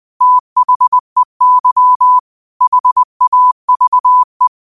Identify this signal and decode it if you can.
Morse code, heard as a keyed tone, about 20 wpm, 1000 Hz. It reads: THEY HAVE
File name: qotdfiles / 20wpm